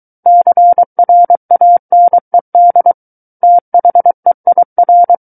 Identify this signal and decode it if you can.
Morse code, heard as a keyed tone, about 23 wpm, 700 Hz. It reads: CRANED THEIR
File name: Book One / 14-Chapter 14 - 23 WPM